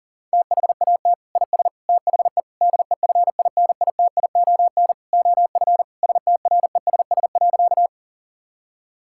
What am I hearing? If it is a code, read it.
Morse code, heard as a keyed tone, about 40 wpm, 700 Hz. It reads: THAT IS THE DEFINITION OF STRESS.
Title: Sentences from Top 1000 Words - Review Part 6 - 40wpm